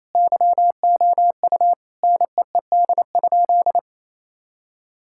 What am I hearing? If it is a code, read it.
Morse code, heard as a keyed tone, about 28 wpm, 700 Hz. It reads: YOU NEED?
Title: Sentences from Top 1000 Words - Review Part 5 - 28wpm